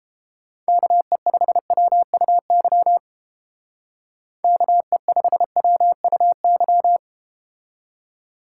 Morse code, heard as a keyed tone, about 32 wpm, 700 Hz. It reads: KE5WUY KE5WUY